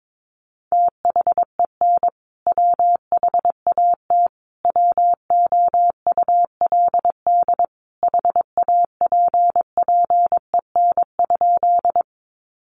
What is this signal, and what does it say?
Morse code, heard as a keyed tone, about 22 wpm, 700 Hz. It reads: THEN WHAT WOULD HAPPEN?
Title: Sentences from Top 400 Words 22wpm